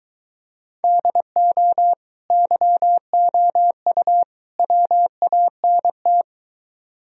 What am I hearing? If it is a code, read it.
Morse code, heard as a keyed tone, about 23 wpm, 700 Hz. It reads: DO YOU WANT